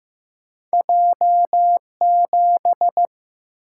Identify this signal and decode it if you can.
Morse code, heard as a keyed tone, about 15 wpm, 700 Hz. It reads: J 7
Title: Single-Letter Single-Number 15wpm